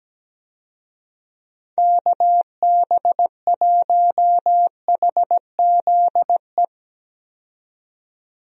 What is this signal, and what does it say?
Morse code, heard as a keyed tone, about 17 wpm, 700 Hz. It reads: KB1HZE
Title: USA Call Signs - 6 Characters - 17wpm